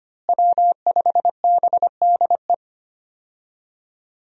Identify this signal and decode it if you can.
Morse code, heard as a keyed tone, about 25 wpm, 700 Hz. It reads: W5BDE
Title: USA Call Signs - 5 Characters - Repeated 2x - 25wpm